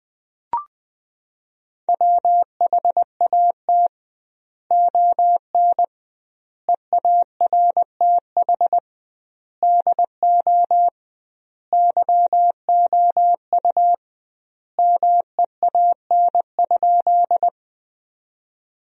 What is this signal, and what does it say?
Morse code, heard as a keyed tone, about 20 wpm, 700 Hz. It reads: WHAT ON EARTH DO YOU MEAN?
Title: 2x Word-Spacing - Sentences from Top 200 Words - 20wpm